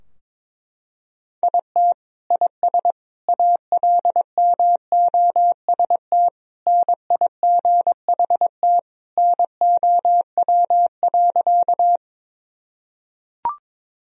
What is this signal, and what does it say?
Morse code, heard as a keyed tone, about 22 wpm, 700 Hz. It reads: IT IS ALMOST NIGHT NOW.
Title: Sentences from Top 1000 Words - Review Part 4 - 22wpm